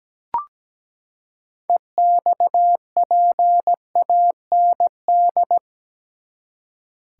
Morse code, heard as a keyed tone, about 17 wpm, 700 Hz. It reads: EXPAND
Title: Six Letter Words - 17wpm